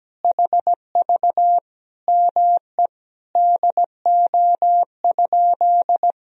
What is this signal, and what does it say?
Morse code, heard as a keyed tone, about 17 wpm, 700 Hz. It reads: HV ME DO?
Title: Sentences from Top 200 Words and Abbreviations 17wpm